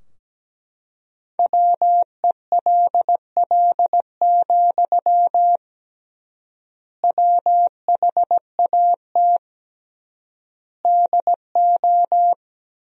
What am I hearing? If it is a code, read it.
Morse code, heard as a keyed tone, about 17 wpm, 700 Hz. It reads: WELL, WHAT DO
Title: 3x Word-Spacing - Sentences from Top 200 Words - 17wpm